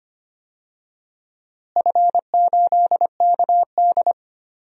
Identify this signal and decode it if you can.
Morse code, heard as a keyed tone, about 25 wpm, 700 Hz. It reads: F8KD
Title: DXCC-100 Call Signs - 4 Characters - Repeated 2x - 25wpm